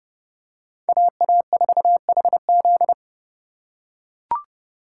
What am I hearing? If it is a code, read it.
Morse code, heard as a keyed tone, about 27 wpm, 700 Hz. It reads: AA4HZ